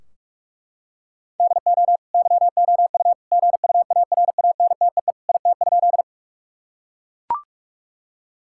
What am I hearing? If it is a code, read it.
Morse code, heard as a keyed tone, about 41 wpm, 700 Hz. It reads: DO YOU GUARANTEE IT?
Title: Sentences from Top 1600 Words 45wpm